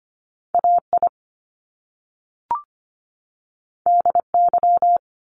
Morse code, heard as a keyed tone, about 25 wpm, 700 Hz. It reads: AI DY
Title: Top Two Letter Combinations in Words 25wpm